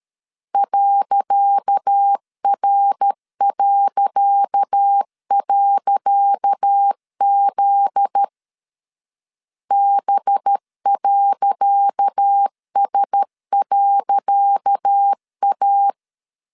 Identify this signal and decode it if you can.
Morse code, heard as a keyed tone, about 13 wpm, 800 Hz. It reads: .R..Z B.S.A
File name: LEZIONI / 10 SIGNS / 10C DOT 12wpm 2451